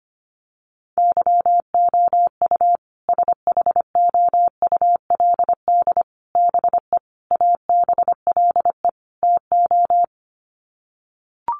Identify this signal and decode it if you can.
Morse code, heard as a keyed tone, about 25 wpm, 700 Hz. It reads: YOU SHOULD BE ABLE TO
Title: Sets of 5 Words Encore 25wpm